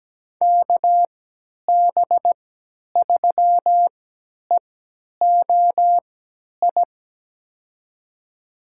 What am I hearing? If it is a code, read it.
Morse code, heard as a keyed tone, about 17 wpm, spaced as for 10 wpm, 700 Hz. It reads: KB3EOI
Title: USA Call Signs - 6 Characters - 3x Character Spacing - 17wpm